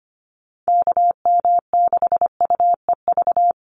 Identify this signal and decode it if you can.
Morse code, heard as a keyed tone, about 25 wpm, 700 Hz. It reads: KM6UEV
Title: USA Call Signs 25wpm